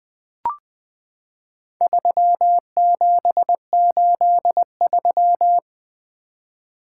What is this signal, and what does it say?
Morse code, heard as a keyed tone, about 20 wpm, 700 Hz. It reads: 3783
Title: Four Letter-Number Combinations - 20wpm